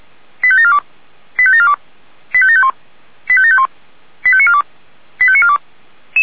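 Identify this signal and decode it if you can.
Five-tone selective call (ZVEI), heard as quick runs of five tones, tones of about 70 ms. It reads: 85742 85742 85751 85751 85942 85942
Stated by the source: Five-tone ZVEI selective call received on an SDR.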